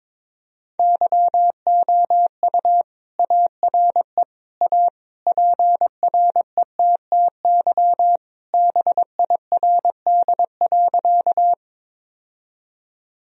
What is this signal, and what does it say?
Morse code, heard as a keyed tone, about 22 wpm, 700 Hz. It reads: YOU ARE A PRETTY BIRD.
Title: Sentences from Top 1000 Words - Review Part 2 - 22wpm